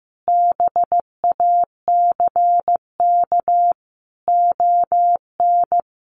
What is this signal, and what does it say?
Morse code, heard as a keyed tone, about 15 wpm, 700 Hz. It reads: BACK ON
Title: Sentences from Top 1200 Words 15wpm